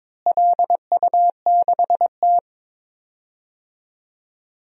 Morse code, heard as a keyed tone, about 22 wpm, 700 Hz. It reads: LU6T
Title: DXCC-100 Call Signs - 4 Characters - Repeated 2x - 22wpm